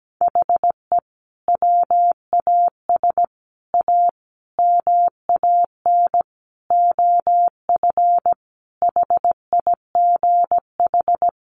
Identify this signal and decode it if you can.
Morse code, heard as a keyed tone, about 17 wpm, 700 Hz. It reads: HE WAS A MAN OF HIGH
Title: Sentences from Top 1200 Words 17wpm